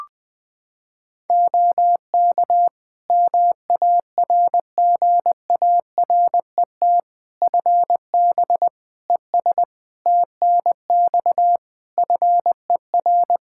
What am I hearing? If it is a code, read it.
Morse code, heard as a keyed tone, about 21 wpm, 700 Hz. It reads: OK MARGARET FB ES TNX FER
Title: Practice QSO - Exchange 2 - Info - 20wpm